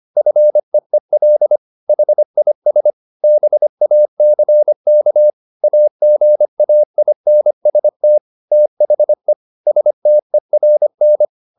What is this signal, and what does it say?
Morse code, heard as a keyed tone, about 25 wpm, 600 Hz. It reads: FEEL HIS BACK AGAINST THE STERN